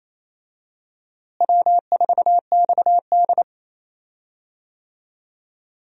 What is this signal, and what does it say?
Morse code, heard as a keyed tone, about 28 wpm, 700 Hz. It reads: W4XD
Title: USA Call Signs - 4 Characters - Repeated 3x - 28wpm